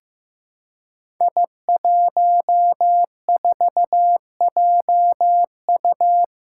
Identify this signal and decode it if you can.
Morse code, heard as a keyed tone, about 15 wpm, 700 Hz. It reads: I14JU
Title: Five Challenging Letter-Number Combinations 15wpm